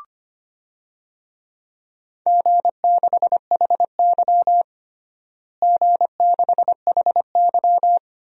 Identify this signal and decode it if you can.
Morse code, heard as a keyed tone, about 25 wpm, 700 Hz. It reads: G6HY G6HY
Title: DXCC-100 Call Signs - 4 Characters - Repeated 2x - 25wpm